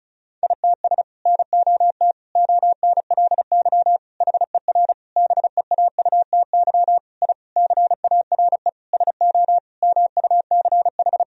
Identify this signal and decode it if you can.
Morse code, heard as a keyed tone, about 36 wpm, 700 Hz. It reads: ITS NOT ONLY HER BEAUTY I CARE SO MUCH
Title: Sentences from Top 1000 Words - Review Part 2 - 35wpm